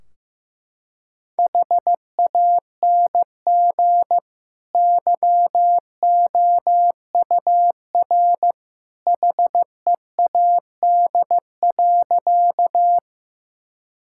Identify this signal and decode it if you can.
Morse code, heard as a keyed tone, about 15 wpm, 700 Hz. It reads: HANG YOUR HEAD.
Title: Sentences from Top 1600 Words 15wpm